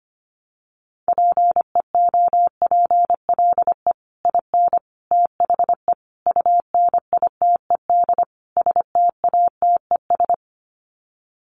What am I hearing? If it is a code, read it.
Morse code, heard as a keyed tone, about 25 wpm, 700 Hz. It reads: PEOPLE IN THE UNITED STATES
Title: Sets of 5 Words Encore 25wpm